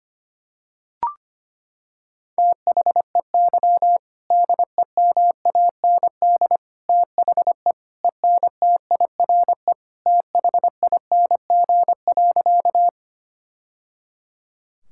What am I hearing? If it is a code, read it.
Morse code, heard as a keyed tone, about 25 wpm, 700 Hz. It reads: THEY DEMAND THE ENTIRE THING.
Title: Sentences from Top 1000 Words - Review Part 5 - 25wpm